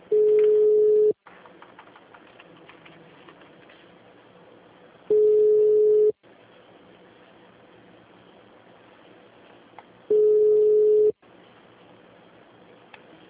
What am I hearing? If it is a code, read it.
Ringback tone (Europe, 425 Hz).
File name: out- -2016- - - .133